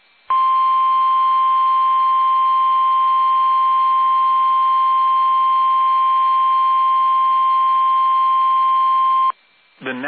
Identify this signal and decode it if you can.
Emergency Alert System (SAME) tones (the 1050 Hz attention tone). It is heard from 0.4 to 9.2 s.
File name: eas / KUSO / KUSO